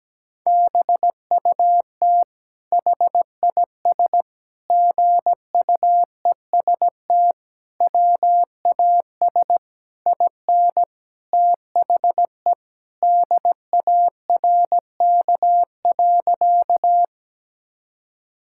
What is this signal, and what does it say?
Morse code, heard as a keyed tone, about 17 wpm, 700 Hz. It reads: BUT HIS GUEST WAS IN THE DARK.
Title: Sentences from Top 1600 Words 17wpm